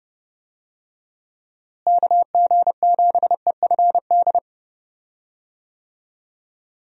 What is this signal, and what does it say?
Morse code, heard as a keyed tone, about 32 wpm, 700 Hz. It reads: KG7EFD
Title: USA Call Signs - 6 Characters - Repeated 3x - 30wpm